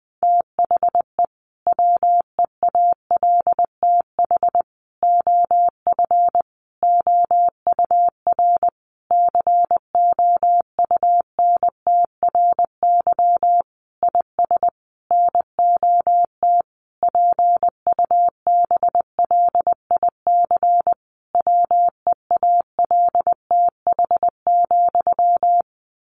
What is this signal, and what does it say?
Morse code, heard as a keyed tone, about 20 wpm, 700 Hz. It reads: THE WEALTH OF OUR COUNTRY IS NOT PUBLIC WEALTH,
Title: Sentences from Top 1200 Words 20wpm